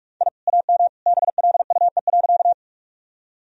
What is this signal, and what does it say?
Morse code, heard as a keyed tone, about 42 wpm, 700 Hz. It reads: I AM BLUE.
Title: Sentences from Top 1000 Words - Review Part 3 - 45wpm